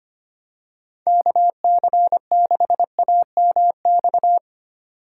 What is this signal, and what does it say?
Morse code, heard as a keyed tone, about 25 wpm, 700 Hz. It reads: KC6AMX
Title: USA Call Signs - 6x Character Spacing - 25wpm